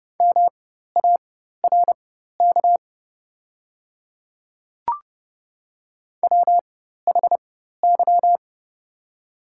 Morse code, heard as a keyed tone, about 30 wpm, spaced as for 14 wpm, 700 Hz. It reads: MARK WHY
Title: 4x Character Spacing - Top 300 Words - 30wpm